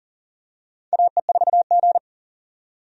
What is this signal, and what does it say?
Morse code, heard as a keyed tone, about 40 wpm, 700 Hz. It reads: AE4G
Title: USA Call Signs - 4 Characters - 40wpm